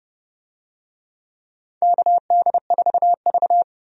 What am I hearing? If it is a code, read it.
Morse code, heard as a keyed tone, about 29 wpm, 700 Hz. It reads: KD4V
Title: USA Call Signs - 4 Characters - 30wpm